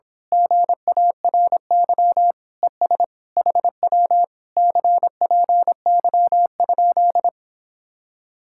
Morse code, heard as a keyed tone, about 26 wpm, 700 Hz. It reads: GARY ES HW CPY?
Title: Speed-Racing Common QSO Phrases - 22wpm